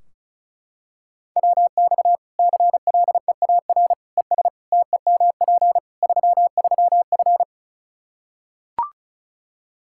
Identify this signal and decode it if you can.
Morse code, heard as a keyed tone, about 34 wpm, 700 Hz. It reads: WX CLEAR ES TEMP 33F
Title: Practice QSO - Exchange 2 - Info - 35wpm